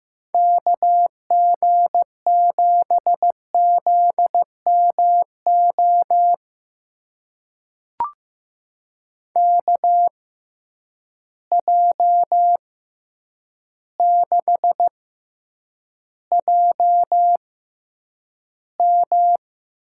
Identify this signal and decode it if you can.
Morse code, heard as a keyed tone, about 15 wpm, spaced as for 8 wpm, 700 Hz. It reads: KG7ZMO KJ6JM
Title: USA Call Signs - 6 Characters - 6x Character Spacing - 15wpm